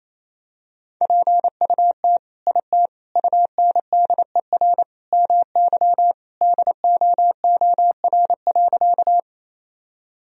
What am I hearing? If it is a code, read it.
Morse code, heard as a keyed tone, about 28 wpm, 700 Hz. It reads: PUT IT UNDER MY DOOR.
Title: Sentences from Top 1000 Words - Review Part 1 - 28wpm